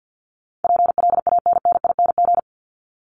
Morse code, heard as a keyed tone, about 50 wpm, 700 Hz. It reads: PLANNING
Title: ING Suffix - Top 100 Words with Base Words 50wpm